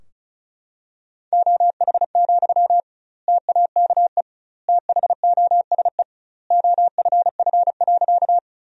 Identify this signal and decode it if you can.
Morse code, heard as a keyed tone, about 36 wpm, 700 Hz. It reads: OH, TAKE THOSE OFF.
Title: 2x Word-Spacing - Sentences from Top 500 Words - 35wpm